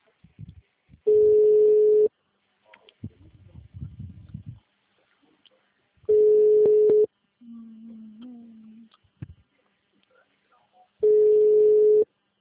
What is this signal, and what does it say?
Ringback tone (Europe, 425 Hz), with speech.
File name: out- -2012- - - .47617